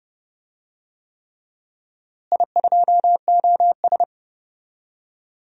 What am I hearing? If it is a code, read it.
Morse code, heard as a keyed tone, about 32 wpm, 700 Hz. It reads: I2OS
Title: DXCC-100 Call Signs - 4 Characters - Repeated 3x - 30wpm